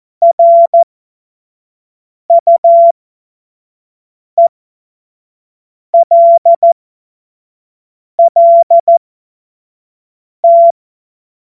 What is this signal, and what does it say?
Morse code, heard as a keyed tone, about 14 wpm, spaced as for 5 wpm, 680 Hz. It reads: RUELLT
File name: Morsekurs / Witze / Witz 17. Januar 2023 25 F70